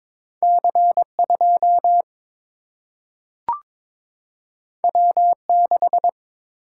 Morse code, heard as a keyed tone, about 22 wpm, 700 Hz. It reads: C 2 W 6
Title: Single-Letter Single-Number 22wpm